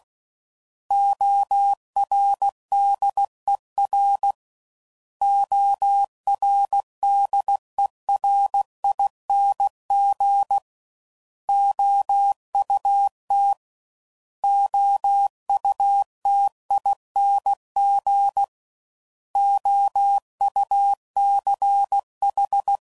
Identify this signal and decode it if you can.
Morse code, heard as a keyed tone, about 16 wpm, 800 Hz. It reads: ORDER ORDERING OUT OUTING OUCH
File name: Prefixes / By First Letter / Os